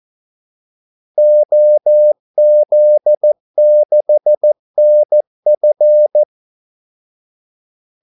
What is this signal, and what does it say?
Morse code, heard as a keyed tone, about 14 wpm, 600 Hz. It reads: OZ6NF
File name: Callsigns-14wpm